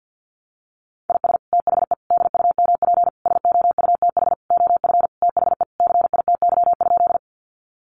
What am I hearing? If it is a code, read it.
Morse code, heard as a keyed tone, about 49 wpm, 700 Hz. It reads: IS THE DUMP SOUTH OF THE CITY?
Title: Sentences from Top 1300 Words 50wpm